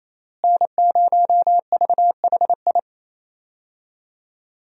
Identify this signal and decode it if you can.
Morse code, heard as a keyed tone, about 28 wpm, 700 Hz. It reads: N0VHI
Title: USA Call Signs - 5 Characters - Repeated 2x - 28wpm